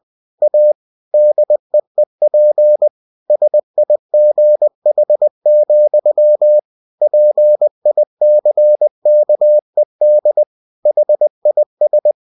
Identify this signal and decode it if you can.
Morse code, heard as a keyed tone, about 20 wpm, 600 Hz. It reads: A DEEP SIGH, PICKED HIS
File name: cw-books / winnie-the-pooh / winnie-20wpm-page-0110